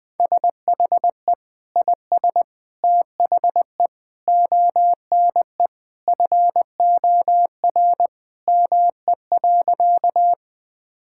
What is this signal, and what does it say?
Morse code, heard as a keyed tone, about 20 wpm, 700 Hz. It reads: SHE IS THE ONE FOR ME.
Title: Sentences with 2-Word Phrases - 20wpm